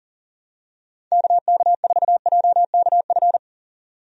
Morse code, heard as a keyed tone, about 40 wpm, 700 Hz. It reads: KK4JKF
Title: USA Call Signs - 6 Characters - 40wpm